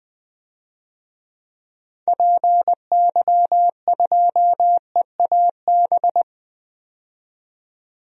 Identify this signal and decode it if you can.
Morse code, heard as a keyed tone, about 20 wpm, 700 Hz. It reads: PY2EAB
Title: DXCC-100 Call Signs - 6 Characters - 20wpm